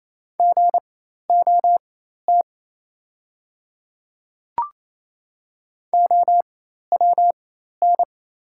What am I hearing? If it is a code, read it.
Morse code, heard as a keyed tone, about 27 wpm, spaced as for 13 wpm, 700 Hz. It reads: GOT OWN
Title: 4x Character Spacing - Top 300 Words - 28wpm